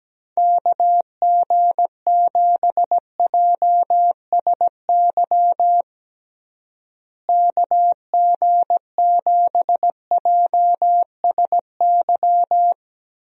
Morse code, heard as a keyed tone, about 17 wpm, 700 Hz. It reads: KG7JSY KG7JSY
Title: USA Call Signs - 6 Characters - Repeated 2x - 17wpm